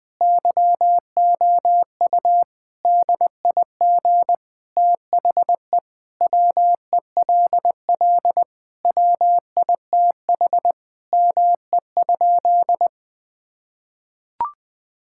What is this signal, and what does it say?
Morse code, heard as a keyed tone, about 20 wpm, 700 Hz. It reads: YOU DIG THE WELL WITH ME?
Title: Sentences from Top 1600 Words 20wpm